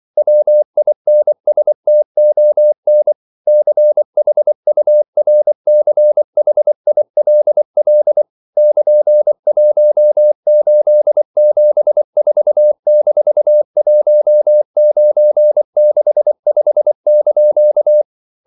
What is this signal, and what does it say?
Morse code, heard as a keyed tone, about 24 wpm, 600 Hz. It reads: WINSTON CHURCHILL <KN>1874-1965)
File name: CW / Practice / EnglishQuotations0424